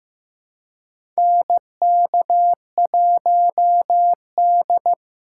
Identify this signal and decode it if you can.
Morse code, heard as a keyed tone, about 15 wpm, 700 Hz. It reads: NK1D
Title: USA Call Signs - 4 Characters - 3x Character Spacing - 15wpm